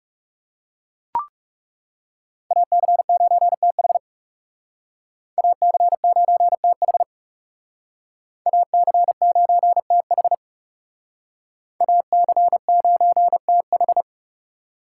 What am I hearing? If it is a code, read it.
Morse code, heard as a keyed tone, about 36 wpm, 700 Hz. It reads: AC9TH AC9TH AC9TH AC9TH
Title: Speed Racing - USA 2x2 Call Signs 30wpm